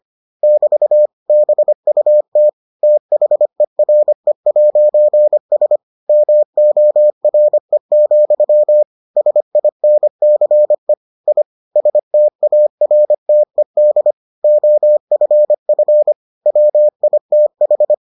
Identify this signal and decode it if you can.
Morse code, heard as a keyed tone, about 25 wpm, 600 Hz. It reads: <BT> BUT THERE'S MORE, SINCE I STARTED OFF WITH